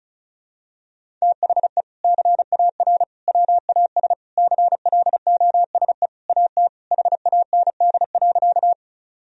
Morse code, heard as a keyed tone, about 36 wpm, 700 Hz. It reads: THE CAR WAS CLOSE AT HAND.
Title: Sentences from Top 1000 Words - Review Part 1 - 35wpm